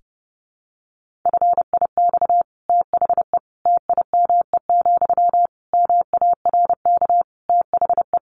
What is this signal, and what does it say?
Morse code, heard as a keyed tone, about 28 wpm, 700 Hz. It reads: FIX THE TIME, MARK THE
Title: Sentences from Top 1400 Words 30wpm